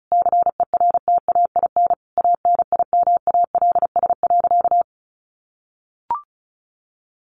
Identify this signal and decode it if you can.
Morse code, heard as a keyed tone, about 34 wpm, 700 Hz. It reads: CERTAIN ANIMALS.
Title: A Sound Of Thunder 35wpm